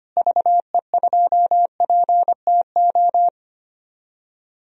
Morse code, heard as a keyed tone, about 25 wpm, 700 Hz. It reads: VE2PTO
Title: DXCC-100 Call Signs - 6 Characters - Repeated 3x - 25wpm